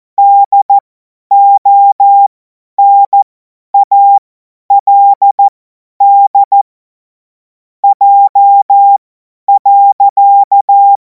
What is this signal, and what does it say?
Morse code, heard as a keyed tone, about 14 wpm, spaced as for 10 wpm, 800 Hz. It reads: DONALD J.
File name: Morsekurs / Witze / Witz 12. Mai 2026 50 F70